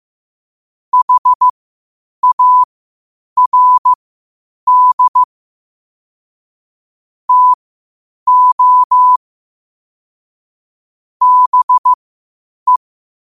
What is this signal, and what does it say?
Morse code, heard as a keyed tone, about 15 wpm, spaced as for 8 wpm, 1000 Hz. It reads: HARD TO BE
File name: qotdfiles / 8wpm